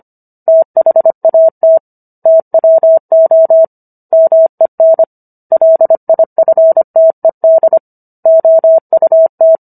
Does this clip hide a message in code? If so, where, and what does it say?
Yes: Morse code from 0.5 to 9.6 s, heard as a keyed tone, about 25 wpm, 650 Hz. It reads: THAT TWO MEN LIFTED OUT